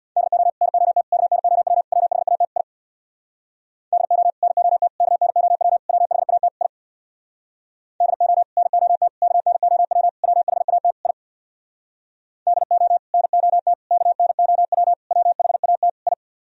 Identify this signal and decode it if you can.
Morse code, heard as a keyed tone, about 64 wpm, 700 Hz. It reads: DO NOT KNOW WHAT I DO NOT KNOW WHAT I DO NOT KNOW WHAT I DO NOT KNOW WHAT I